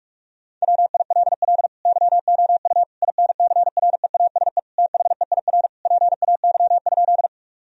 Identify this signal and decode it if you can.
Morse code, heard as a keyed tone, about 45 wpm, 700 Hz. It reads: WILL YOU INCREASE THEIR PAY?
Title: Sentences from Top 1000 Words - Review Part 4 - 45wpm